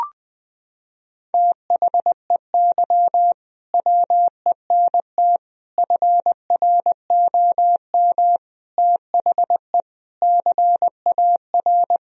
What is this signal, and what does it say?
Morse code, heard as a keyed tone, about 20 wpm, 700 Hz. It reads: THEY WENT FROM THE CAR
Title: Sentences from Top 1000 Words - Review Part 1 - 20wpm